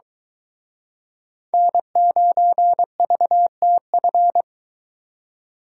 Morse code, heard as a keyed tone, about 23 wpm, 700 Hz. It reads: N9VTF